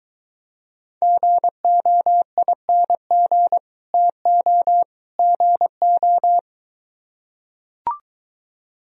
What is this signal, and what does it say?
Morse code, heard as a keyed tone, about 23 wpm, 700 Hz. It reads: GOING TO GO